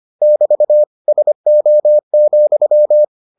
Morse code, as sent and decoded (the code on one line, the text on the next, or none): -...- / ... --- --..--
<BT> SO,